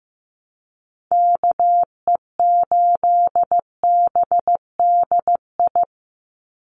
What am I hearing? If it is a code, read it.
Morse code, heard as a keyed tone, about 15 wpm, 700 Hz. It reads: KE8BDI
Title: USA Call Signs 15wpm